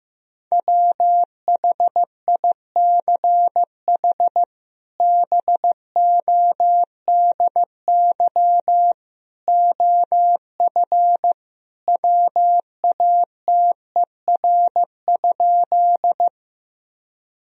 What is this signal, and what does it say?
Morse code, heard as a keyed tone, about 15 wpm, 700 Hz. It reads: WHICH BODY OF WATER?
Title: Sentences from Top 1000 Words - Review Part 2 - 15wpm